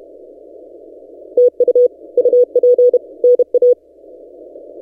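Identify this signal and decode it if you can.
Morse code, heard as a keyed tone, about 31 wpm, 490 Hz. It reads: TU UP NA
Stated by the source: Morse from DXpedition station A35X on 12 m, posted by N6ML.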